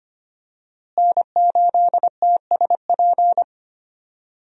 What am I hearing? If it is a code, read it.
Morse code, heard as a keyed tone, about 25 wpm, 700 Hz. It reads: N8TSP
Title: USA Call Signs - 5 Characters - Repeated 2x - 25wpm